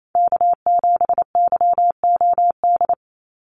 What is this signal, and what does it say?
Morse code, heard as a keyed tone, about 28 wpm, 700 Hz. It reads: K7YOD
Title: USA Call Signs 28wpm